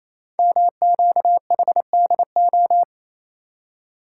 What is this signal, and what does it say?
Morse code, heard as a keyed tone, about 29 wpm, 700 Hz. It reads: MQHDO
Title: Code Groups - 5 Letters - 28wpm